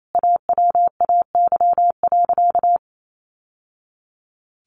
Morse code, heard as a keyed tone, about 28 wpm, 700 Hz. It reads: AWAY.
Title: Sentences from Top 600 Words 28wpm